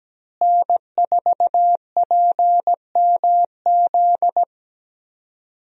Morse code, heard as a keyed tone, about 17 wpm, 700 Hz. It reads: N4PMZ
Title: USA Call Signs - 5 Characters - Repeated 3x - 17wpm